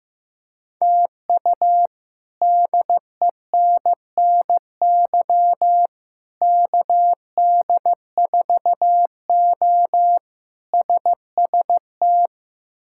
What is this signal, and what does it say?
Morse code, heard as a keyed tone, about 15 wpm, 700 Hz. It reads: TU DENNY KD4O SST
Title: K1USN SST QSO Practice - 15wpm